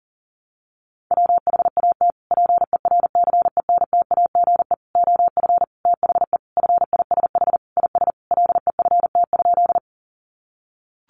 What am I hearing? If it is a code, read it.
Morse code, heard as a keyed tone, about 40 wpm, 700 Hz. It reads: WHAT PERCENTAGE OF THE FISH IS LEFT?
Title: Sentences from Top 1300 Words 40wpm